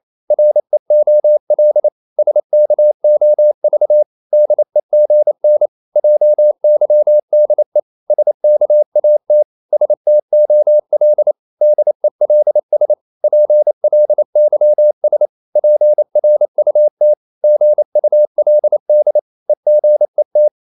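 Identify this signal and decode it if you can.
Morse code, heard as a keyed tone, about 28 wpm, 600 Hz. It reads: REOL SKOV DEGN JYDE SKAT STOL DELS PLYS PRUT GULD EGET
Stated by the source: Danish short words sent in Morse at 28 wpm, from CW med Gnister.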